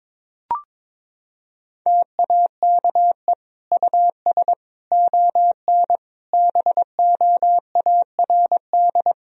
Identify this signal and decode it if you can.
Morse code, heard as a keyed tone, about 22 wpm, 700 Hz. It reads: TAKE US ON BOARD
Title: Sentences from Top 1000 Words - Review Part 4 - 22wpm